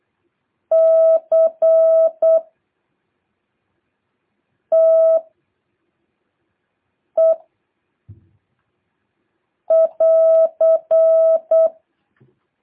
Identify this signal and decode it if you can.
Morse code, heard as a keyed tone, about 8 wpm, 650 Hz. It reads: C T E <AR>
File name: jh8cbh / 2023cw